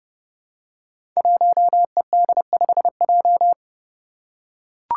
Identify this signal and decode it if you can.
Morse code, heard as a keyed tone, about 30 wpm, 700 Hz. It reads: 1ED5J
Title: Five Challenging Letter-Number Combinations 30wpm